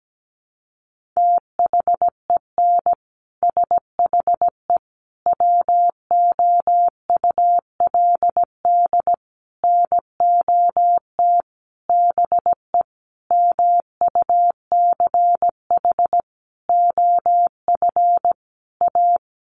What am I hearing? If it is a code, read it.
Morse code, heard as a keyed tone, about 17 wpm, 700 Hz. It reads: THEN SHE WOULD NOT BE MUCH OF A
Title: Sentences from Top 1400 Words 17wpm